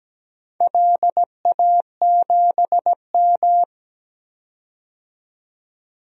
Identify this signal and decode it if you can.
Morse code, heard as a keyed tone, about 17 wpm, 700 Hz. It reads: LA7M
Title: DXCC-100 Call Signs - 4 Characters - 17wpm